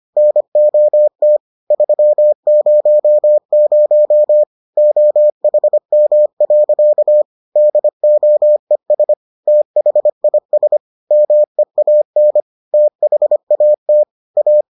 Morse code, heard as a keyed tone, about 25 wpm, 600 Hz. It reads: NOT 300 OHM. DOES THIS MEAN THAT A